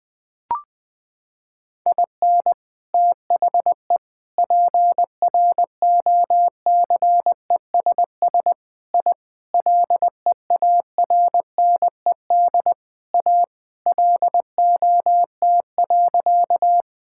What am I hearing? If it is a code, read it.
Morse code, heard as a keyed tone, about 20 wpm, 700 Hz. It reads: IN THE PROCESS I LEARNED A LOT.
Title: Sentences with 3-Word Phrases - 20wpm